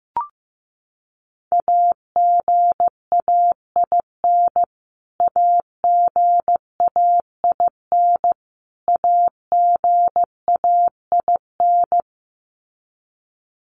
Morse code, heard as a keyed tone, about 15 wpm, 700 Hz. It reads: AGAIN AGAIN AGAIN
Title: Top 300 Words Repeated 15wpm